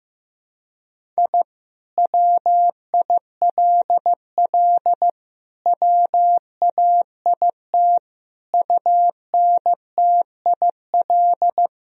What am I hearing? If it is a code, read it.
Morse code, heard as a keyed tone, about 15 wpm, 700 Hz. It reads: I WILL WAIT UNTIL
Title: Sentences from Top 1000 Words - Review Part 1 - 15wpm